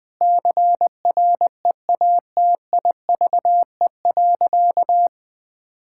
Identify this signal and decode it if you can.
Morse code, heard as a keyed tone, about 20 wpm, 700 Hz. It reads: CREATIVE.
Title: Sentences from Top 1000 Words - Review Part 5 - 20wpm